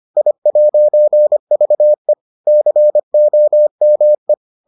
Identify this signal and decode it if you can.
Morse code, heard as a keyed tone, about 25 wpm, 600 Hz. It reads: I'VE COME